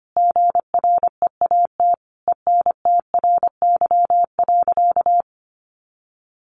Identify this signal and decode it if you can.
Morse code, heard as a keyed tone, about 25 wpm, 700 Hz. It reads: GREAT ENTRY.
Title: Sentences from Top 1500 Words 25wpm